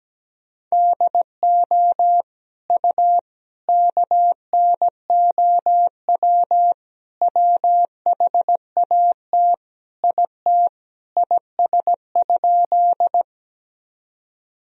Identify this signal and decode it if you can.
Morse code, heard as a keyed tone, about 17 wpm, 700 Hz. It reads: DO U KNOW WHAT IT IS?
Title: Sentences from Top 100 Words and Abbreviations 17wpm